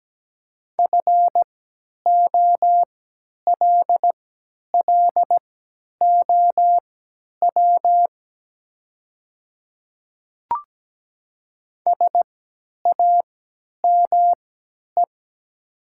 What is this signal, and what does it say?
Morse code, heard as a keyed tone, about 17 wpm, spaced as for 10 wpm, 700 Hz. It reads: FOLLOW SAME
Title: 3x Character Spacing - Top 200 Words - 17wpm